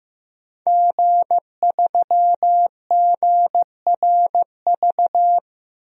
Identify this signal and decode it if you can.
Morse code, heard as a keyed tone, about 15 wpm, 700 Hz. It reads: G3GRV
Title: DXCC-100 Call Signs - 5 Characters - Repeated 3x - 15wpm